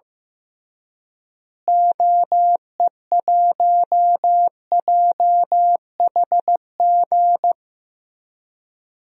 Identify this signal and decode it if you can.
Morse code, heard as a keyed tone, about 15 wpm, 700 Hz. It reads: OE1JHG
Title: DXCC-100 Call Signs - 6 Characters - Repeated 3x - 15wpm